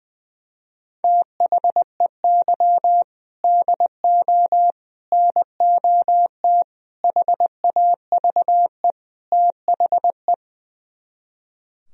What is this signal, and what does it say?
Morse code, heard as a keyed tone, about 20 wpm, 700 Hz. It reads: THEY DO NOT HAVE THE
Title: Speed-Racing - Sets of 5 Words 20wpm